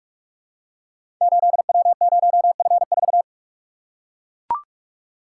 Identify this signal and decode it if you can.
Morse code, heard as a keyed tone, about 44 wpm, 700 Hz. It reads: 8W0F4
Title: Code Groups - 5 Letter:Numbers - 45wpm